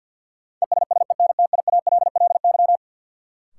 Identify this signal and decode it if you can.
Morse code, heard as a keyed tone, about 49 wpm, 700 Hz. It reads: ESSENTIALLY
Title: Sentences from Top 1300 Words - New Word Drill - 50wpm